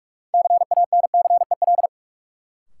Morse code, heard as a keyed tone, about 46 wpm, 700 Hz. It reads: CANCEL
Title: Sentences from Top 1200 Words - New Word Drill - 45wpm